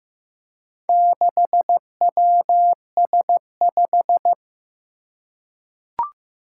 Morse code, heard as a keyed tone, about 15 wpm, 700 Hz. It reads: 6WS5
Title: Four Challenging Letter-Number Combinations 15wpm